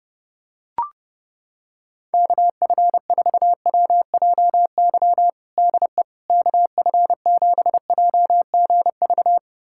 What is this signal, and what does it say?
Morse code, heard as a keyed tone, about 32 wpm, 700 Hz. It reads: KF4WJY DE KF7JGV
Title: USA Callsign DE Callsign 30wpm